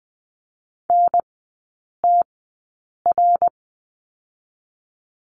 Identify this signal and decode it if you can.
Morse code, heard as a keyed tone, about 20 wpm, spaced as for 8 wpm, 700 Hz. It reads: NTR
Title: 053 Lesson 10e - Letter r - Three Characters 20wpm